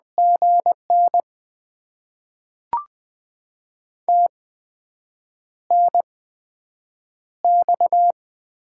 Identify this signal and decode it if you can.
Morse code, heard as a keyed tone, about 20 wpm, spaced as for 7 wpm, 700 Hz. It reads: GN TNX
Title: Top 20 QSO Elements - 8x Character Spacing - 20wpm